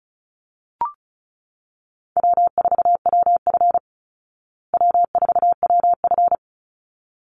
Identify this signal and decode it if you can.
Morse code, heard as a keyed tone, about 34 wpm, 700 Hz. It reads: W4WF W4WF
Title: Easy Call Signs Repeated 2x - 35wpm